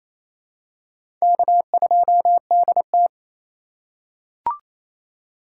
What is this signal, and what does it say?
Morse code, heard as a keyed tone, about 28 wpm, 700 Hz. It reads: K2DT
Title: USA Call Signs - 4 Characters - 2x Character Spacing - 28wpm